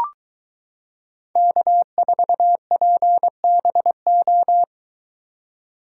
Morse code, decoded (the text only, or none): K4PBO